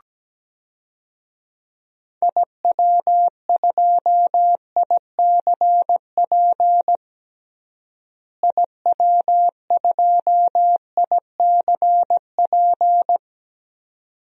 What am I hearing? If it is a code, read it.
Morse code, heard as a keyed tone, about 17 wpm, 700 Hz. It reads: IW2ICP IW2ICP
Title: DXCC-100 Call Signs - 6 Characters - Repeated 2x - 17wpm